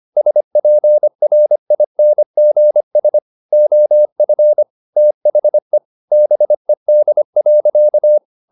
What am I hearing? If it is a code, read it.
Morse code, heard as a keyed tone, about 25 wpm, 600 Hz. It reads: SPRINGS OF THE BED.
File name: the old man and the sea part 1 25wpm